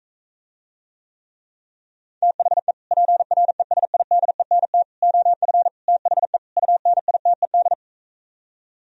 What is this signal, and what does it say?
Morse code, heard as a keyed tone, about 45 wpm, 700 Hz. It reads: THE PRESIDENT OF THE UNITED